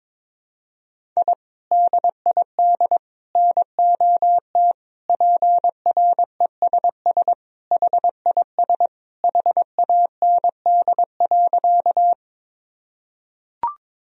Morse code, heard as a keyed tone, about 22 wpm, 700 Hz. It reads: I DID NOT PRESS HIS HAND.
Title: Sentences from Top 1000 Words - Review Part 1 - 22wpm